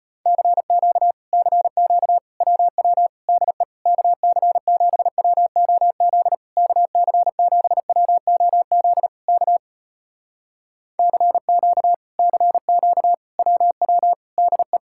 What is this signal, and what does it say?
Morse code, heard as a keyed tone, about 34 wpm, 700 Hz. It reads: CQ CQ WW DE KC7WOZ KC7WOZ K CQ CQ WW DE